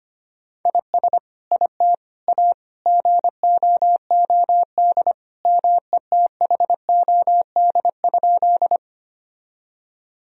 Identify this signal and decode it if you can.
Morse code, heard as a keyed tone, about 25 wpm, 700 Hz. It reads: IS IT A GOOD METHOD?
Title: Sentences from Top 1000 Words - Review Part 3 - 25wpm